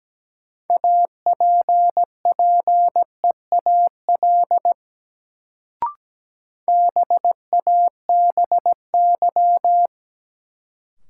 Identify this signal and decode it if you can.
Morse code, heard as a keyed tone, about 17 wpm, 700 Hz. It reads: APPEAL BABY
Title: Sentences from Top 1300 Words - New Word Drill Alt - 17wpm